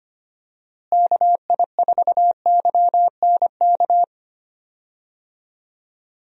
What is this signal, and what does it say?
Morse code, heard as a keyed tone, about 25 wpm, 700 Hz. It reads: KI4YNK
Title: USA Call Signs - 6 Characters - Repeated 3x - 25wpm